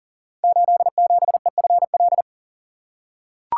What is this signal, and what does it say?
Morse code, heard as a keyed tone, about 42 wpm, 700 Hz. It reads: 87EFL
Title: Code Groups - 5 Letter:Numbers - 40wpm